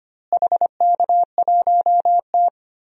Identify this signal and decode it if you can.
Morse code, heard as a keyed tone, about 25 wpm, 700 Hz. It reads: HK1T
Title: DXCC-100 Call Signs - 4 Characters - Repeated 2x - 25wpm